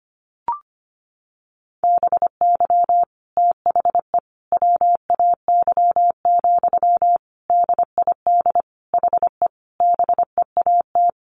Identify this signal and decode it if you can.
Morse code, heard as a keyed tone, about 25 wpm, 700 Hz. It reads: BY THE WAY, DID HE BEAT
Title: Sentences from Top 1300 Words 25wpm